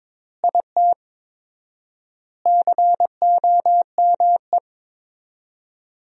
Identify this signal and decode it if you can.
Morse code, heard as a keyed tone, about 22 wpm, 700 Hz. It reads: IT COME
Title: 4x Word-Spacing - Sentences from Top 100 Words - 22wpm